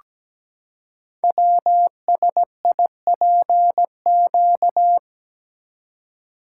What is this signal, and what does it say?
Morse code, heard as a keyed tone, about 17 wpm, 700 Hz. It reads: WSIPQ
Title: Code Groups - 5 Letters - 17wpm